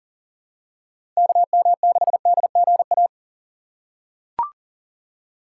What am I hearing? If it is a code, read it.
Morse code, heard as a keyed tone, about 39 wpm, 700 Hz. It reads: KM6DGA
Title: USA Call Signs - 6 Characters - 3x Character Spacing - 40wpm